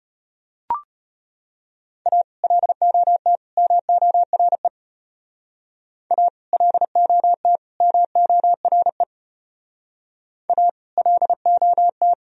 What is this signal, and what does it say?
Morse code, heard as a keyed tone, about 32 wpm, 700 Hz. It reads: A LOT MORE A LOT MORE A LOT